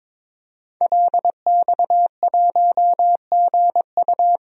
Morse code, heard as a keyed tone, about 22 wpm, 700 Hz. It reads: LX1GU
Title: DXCC-100 Call Signs - 5 Characters - 22wpm